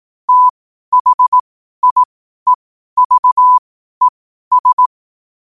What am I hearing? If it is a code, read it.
Morse code, heard as a keyed tone, about 18 wpm, spaced as for 12 wpm, 1000 Hz. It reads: THIEVES
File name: qotdfiles / 12wpm